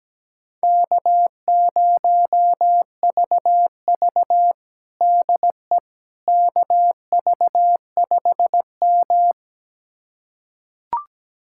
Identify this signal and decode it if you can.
Morse code, heard as a keyed tone, about 17 wpm, 700 Hz. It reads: K0VV DE KV5M
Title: USA Callsign DE Callsign - 4 Characters - 17wpm